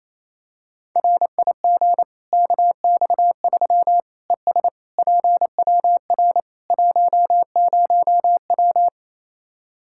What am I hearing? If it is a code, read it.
Morse code, heard as a keyed tone, about 28 wpm, 700 Hz. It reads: RIG KX3 ES PWR 10W
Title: Practice QSO - Exchange 2 - Info - 28wpm